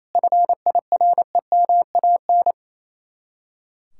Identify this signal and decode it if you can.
Morse code, heard as a keyed tone, about 36 wpm, 700 Hz. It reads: FIREMAN